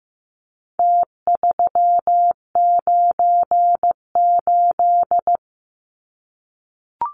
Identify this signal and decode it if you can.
Morse code, heard as a keyed tone, about 15 wpm, 700 Hz. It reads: T398
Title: Numbers - Contest Style - 4-Digit 15wpm